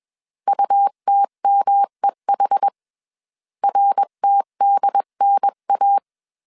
Morse code, heard as a keyed tone, about 21 wpm, 800 Hz. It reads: UTMEH RTDNA